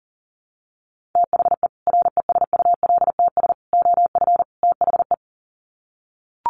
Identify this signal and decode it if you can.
Morse code, heard as a keyed tone, about 40 wpm, 700 Hz. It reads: THE RESULTS OF THE
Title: Sets of 4 Words 40wpm